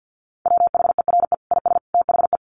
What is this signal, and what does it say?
Morse code, heard as a keyed tone, about 50 wpm, 700 Hz. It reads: WHERE IS THE
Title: Sentences from Top 1300 Words 50wpm